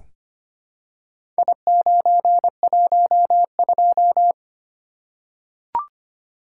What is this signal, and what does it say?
Morse code, heard as a keyed tone, about 26 wpm, 700 Hz. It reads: I912
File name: practice / 25wpm / Four Letter-Number Combinations - 25wpm